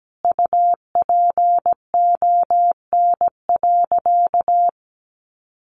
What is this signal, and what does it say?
Morse code, heard as a keyed tone, about 17 wpm, 700 Hz. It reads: UPON.
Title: Sentences from Top 1400 Words 17wpm